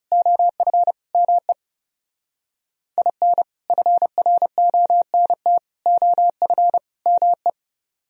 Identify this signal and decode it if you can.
Morse code, heard as a keyed tone, about 35 wpm, 700 Hz. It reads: OF ME IN FRONT OF ME
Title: Speed-Racing - Sets of 4 Words 30wpm